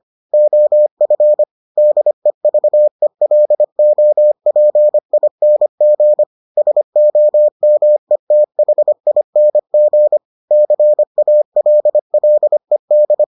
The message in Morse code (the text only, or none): OF DEVELOPING SOMETHING CALLED